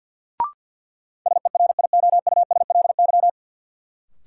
Morse code, heard as a keyed tone, about 52 wpm, 700 Hz. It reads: SERIOUSLY
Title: Sentences from Top 1200 Words - New Word Drill Alt - 50wpm